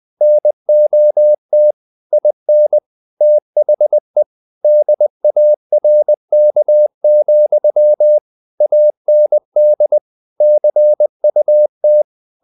Morse code, heard as a keyed tone, about 20 wpm, 600 Hz. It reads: NOT IN THE DARK, AND CUT
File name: the old man and the sea / the old man and the sea part 1 20wpm